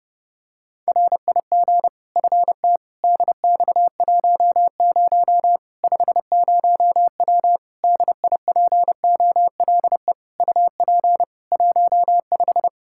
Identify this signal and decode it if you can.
Morse code, heard as a keyed tone, about 30 wpm, 700 Hz. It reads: RIG FT DX10 50W DIPOLE UP 15
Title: K1USN SST QSO Practice 30wpm